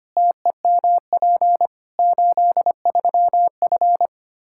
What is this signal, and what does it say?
Morse code, heard as a keyed tone, about 25 wpm, 700 Hz. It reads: TEMP 83F
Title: Practice QSO - Exchange 2 - Info - 25wpm